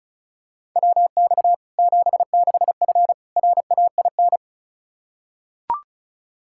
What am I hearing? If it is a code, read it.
Morse code, heard as a keyed tone, about 34 wpm, 700 Hz. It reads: WX 76F RAIN
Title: Practice QSO - Exchange 2 - Info - 35wpm